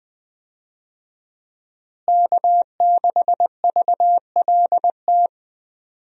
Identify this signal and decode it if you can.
Morse code, heard as a keyed tone, about 20 wpm, 700 Hz. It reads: K6VLT
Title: USA Call Signs - 5 Characters - Repeated 3x - 20wpm